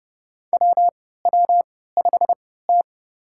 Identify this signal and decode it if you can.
Morse code, heard as a keyed tone, about 30 wpm, spaced as for 17 wpm, 700 Hz. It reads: WW5T
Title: USA Call Signs - 4 Characters - 3x Character Spacing - 30wpm